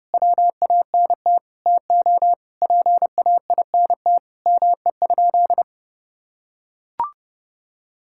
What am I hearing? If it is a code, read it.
Morse code, heard as a keyed tone, about 30 wpm, 700 Hz. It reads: WANT TO PAINT ME?
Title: Sentences from Top 1000 Words - Review Part 6 - 30wpm